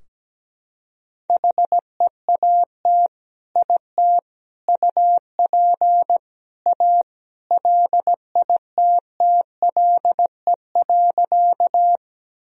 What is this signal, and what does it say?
Morse code, heard as a keyed tone, about 17 wpm, 700 Hz. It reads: HEAT IT UP A LITTLE.
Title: Sentences from Top 1000 Words - Review Part 3 - 17wpm